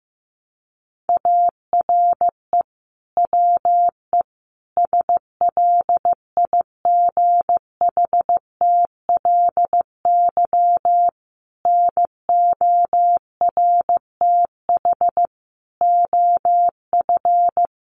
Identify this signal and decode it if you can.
Morse code, heard as a keyed tone, about 15 wpm, 700 Hz. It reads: ARE WE SLIGHTLY NORTH OF
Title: Sentences from Top 1300 Words 15wpm